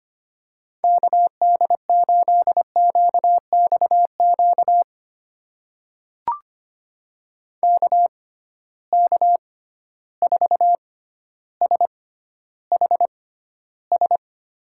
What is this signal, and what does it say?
Morse code, heard as a keyed tone, about 25 wpm, spaced as for 13 wpm, 700 Hz. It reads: KD8QXQ KK4SHS
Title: USA Call Signs - 6 Characters - 6x Character Spacing - 25wpm